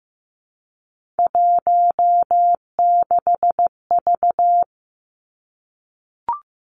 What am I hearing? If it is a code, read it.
Morse code, heard as a keyed tone, about 15 wpm, 700 Hz. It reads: 16V
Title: Three Challenging Letter-Number Combinations 15wpm